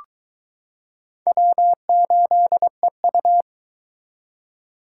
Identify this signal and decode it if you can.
Morse code, heard as a keyed tone, about 22 wpm, 700 Hz. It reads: W8EU